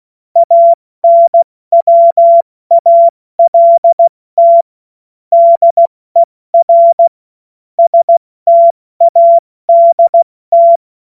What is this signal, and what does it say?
Morse code, heard as a keyed tone, about 16 wpm, 680 Hz. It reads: ANWALT DER STADT
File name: Morsekurs / Witze / Witz 09. August 2022 70 F80